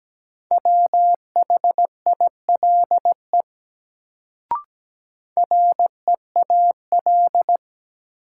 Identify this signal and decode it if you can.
Morse code, heard as a keyed tone, about 17 wpm, 700 Hz. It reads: WHILE REAL
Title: Sentences from Top 300 Words - New Word Drill - 17wpm